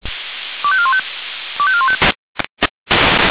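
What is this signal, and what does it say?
Five-tone selective call (ZVEI), heard as quick runs of five tones, tones of about 70 ms. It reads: 25415 25415
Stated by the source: ZVEI five-tone selective call received on an SDR.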